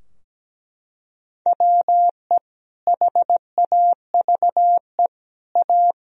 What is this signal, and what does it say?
Morse code, heard as a keyed tone, about 17 wpm, 700 Hz. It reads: WE HAVE A
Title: Sentences from Top 1000 Words - Review Part 3 - 17wpm